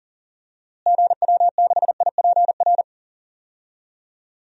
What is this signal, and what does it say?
Morse code, heard as a keyed tone, about 40 wpm, 700 Hz. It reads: GW6IPR
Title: DXCC-100 Call Signs - 6 Characters - Repeated 3x - 40wpm